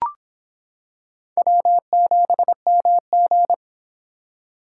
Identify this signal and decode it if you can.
Morse code, heard as a keyed tone, about 26 wpm, 700 Hz. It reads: W7MG